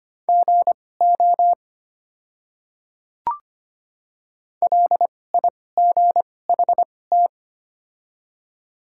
Morse code, heard as a keyed tone, about 25 wpm, spaced as for 18 wpm, 700 Hz. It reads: GO LIGHT
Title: 2x Character Spacing - Top 200 Words - 25wpm